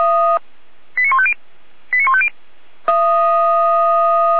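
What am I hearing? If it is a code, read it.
Five-tone selective call (ZVEI), heard as quick runs of five tones, tones of about 70 ms. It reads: 89150 89150
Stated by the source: ZVEI five-tone selective call received on an SDR.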